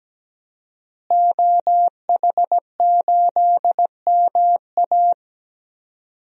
Morse code, heard as a keyed tone, about 17 wpm, 700 Hz. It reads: OH8MA
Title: DXCC-100 Call Signs - 5 Characters - Repeated 3x - 17wpm